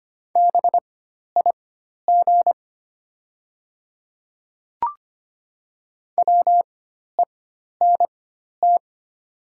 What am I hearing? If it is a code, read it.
Morse code, heard as a keyed tone, about 25 wpm, spaced as for 12 wpm, 700 Hz. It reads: BIG WENT
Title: 4x Character Spacing - Top 200 Words - 25wpm